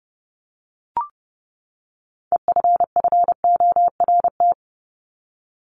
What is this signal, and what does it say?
Morse code, heard as a keyed tone, about 30 wpm, 700 Hz. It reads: EFFORT
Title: Top 1001-1250 Words 30wpm